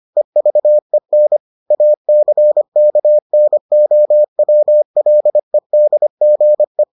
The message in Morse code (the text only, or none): EVEN ACKNOWLEDGE